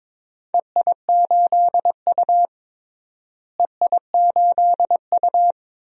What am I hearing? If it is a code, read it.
Morse code, heard as a keyed tone, about 22 wpm, 700 Hz. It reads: EI8U EI8U
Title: DXCC-100 Call Signs - 4 Characters - Repeated 2x - 22wpm